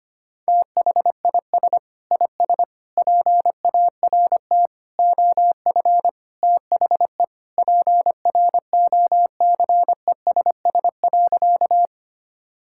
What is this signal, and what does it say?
Morse code, heard as a keyed tone, about 25 wpm, 700 Hz. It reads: THIS IS PART OF THE PROCESS.
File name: practice / 25wpm / Sentences with 2-Word Phrases - 25wpm